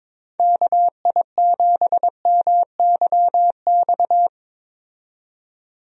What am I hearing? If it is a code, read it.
Morse code, heard as a keyed tone, about 22 wpm, 700 Hz. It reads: KI7MYX
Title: USA Call Signs - 6 Characters - 22wpm